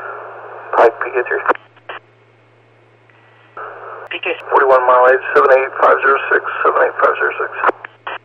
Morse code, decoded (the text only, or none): none